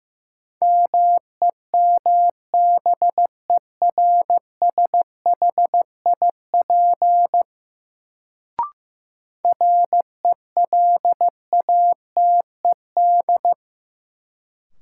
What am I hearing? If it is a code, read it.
Morse code, heard as a keyed tone, about 15 wpm, 700 Hz. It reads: MEMBERSHIP RELATED
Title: Sentences from Top 1400 Words - New Word Drill Alt - 15wpm